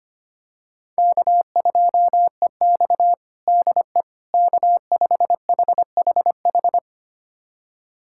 Morse code, heard as a keyed tone, about 25 wpm, 700 Hz. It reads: K2EX DE K5HHH
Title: USA Callsign DE Callsign 25wpm